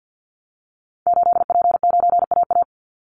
Morse code, heard as a keyed tone, about 50 wpm, 700 Hz. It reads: 8P9AA
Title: Call Signs - 50wpm